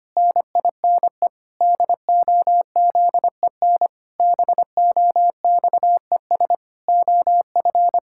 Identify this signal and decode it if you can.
Morse code, heard as a keyed tone, about 25 wpm, 700 Hz. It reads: NINE DOZEN BOXES OF
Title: Pangram Challenge v2 - 25wpm